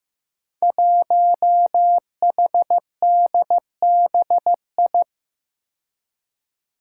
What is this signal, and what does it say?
Morse code, heard as a keyed tone, about 15 wpm, 700 Hz. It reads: 1HDBI
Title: Five Challenging Letter-Number Combinations 15wpm